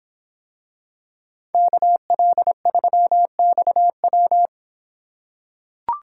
Morse code, heard as a keyed tone, about 26 wpm, 700 Hz. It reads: KL3XW